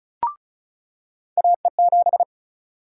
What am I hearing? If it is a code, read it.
Morse code, heard as a keyed tone, about 36 wpm, 700 Hz. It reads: AE7
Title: USA Call Sign Prefixes - 35wpm